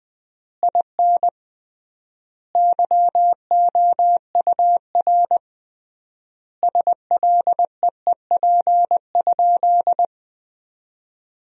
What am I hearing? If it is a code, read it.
Morse code, heard as a keyed tone, about 20 wpm, 700 Hz. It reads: IN YOUR SLEEP?
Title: 3x Word-Spacing - Sentences from Top 500 Words - 20wpm